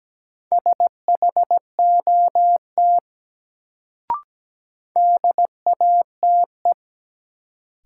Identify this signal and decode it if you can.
Morse code, heard as a keyed tone, about 17 wpm, 700 Hz. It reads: SHOT DATE
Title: Sentences from Top 1100 Words - New Word Drill Alt - 17wpm